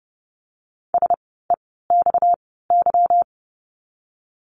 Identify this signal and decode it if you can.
Morse code, heard as a keyed tone, about 30 wpm, spaced as for 17 wpm, 700 Hz. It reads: SEXY
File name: course / 30:10wpm / 230 Lesson 37f - Letter x - Words 30wpm